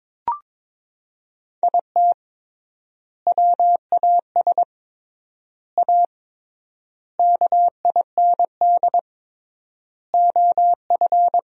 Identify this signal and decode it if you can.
Morse code, heard as a keyed tone, about 22 wpm, 700 Hz. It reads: IT WAS A KIND OF
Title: 3x Word-Spacing - Sentences from Top 500 Words - 22wpm